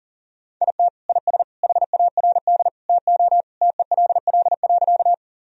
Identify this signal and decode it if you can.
Morse code, heard as a keyed tone, about 40 wpm, 700 Hz. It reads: IT IS HARD TO TELL.
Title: Sentences from Top 1000 Words - Review Part 1 - 40wpm